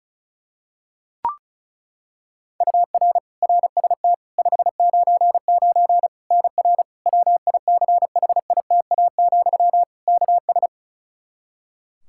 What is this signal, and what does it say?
Morse code, heard as a keyed tone, about 36 wpm, 700 Hz. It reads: UR RST 599 NR WICHITA, KS
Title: K1USN SST QSO Practice 35wpm